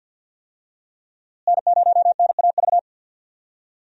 Morse code, heard as a keyed tone, about 53 wpm, 690 Hz. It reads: N0NAV
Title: USA Call Signs - 5 Characters - Repeated 3x - 50wpm